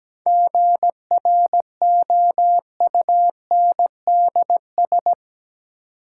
Morse code, heard as a keyed tone, about 17 wpm, 700 Hz. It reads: GROUNDS
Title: Sentences from Top 1300 Words - New Word Drill Alt - 17wpm